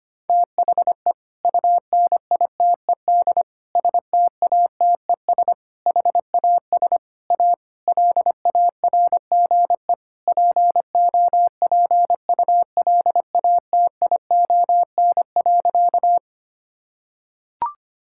Morse code, heard as a keyed tone, about 25 wpm, 700 Hz. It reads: THE UNITED STATES HAS A LARGE POPULATION.
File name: practice / 25wpm / Sentences with 4-Word Phrases - 25wpm